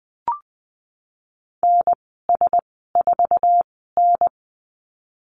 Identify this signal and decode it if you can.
Morse code, heard as a keyed tone, about 20 wpm, spaced as for 15 wpm, 700 Hz. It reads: NS4N
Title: 048 Lesson 9g - Number 4 - Call Signs 20wpm